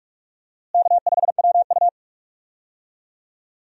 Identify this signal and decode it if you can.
Morse code, heard as a keyed tone, about 46 wpm, 700 Hz. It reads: K5WU
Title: USA Call Signs - 4 Characters - Repeated 3x - 45wpm